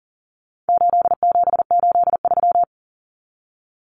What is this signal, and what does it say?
Morse code, heard as a keyed tone, about 40 wpm, 700 Hz. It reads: 8783
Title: Numbers - 4-Digit 40wpm